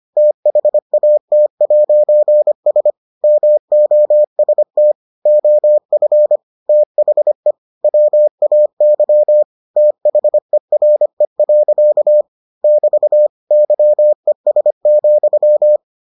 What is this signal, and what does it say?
Morse code, heard as a keyed tone, about 25 wpm, 600 Hz. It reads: THAT'S MOST OF THE WAY THERE. <BT> YES,